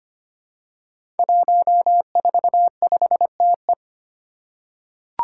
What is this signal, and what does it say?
Morse code, heard as a keyed tone, about 25 wpm, 700 Hz. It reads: 145TE
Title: Five Challenging Letter-Number Combinations 25wpm